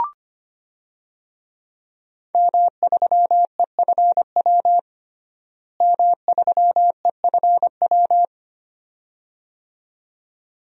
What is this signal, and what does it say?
Morse code, heard as a keyed tone, about 25 wpm, 700 Hz. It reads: M3EFW M3EFW
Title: DXCC-100 Call Signs - 5 Characters - Repeated 2x - 25wpm